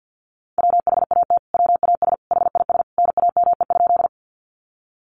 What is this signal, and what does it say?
Morse code, heard as a keyed tone, about 50 wpm, 700 Hz. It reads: WHAT WAS HIS NAME?
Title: Sentences from Top 200 Words 50wpm